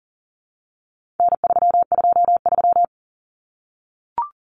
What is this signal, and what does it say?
Morse code, heard as a keyed tone, about 40 wpm, 700 Hz. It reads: N323
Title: Common Cut Numbers - 4-Digit 40wpm